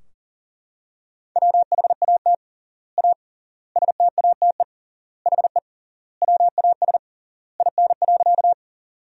Morse code, heard as a keyed tone, about 40 wpm, 700 Hz. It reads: WHAT A STATE HE WAS IN.
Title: 3x Word-Spacing - Sentences from Top 400 Words - 40wpm